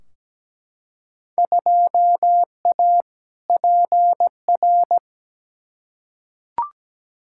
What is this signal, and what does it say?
Morse code, heard as a keyed tone, about 17 wpm, 700 Hz. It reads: 2A PR
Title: Field Day Exchanges Encore - 17wpm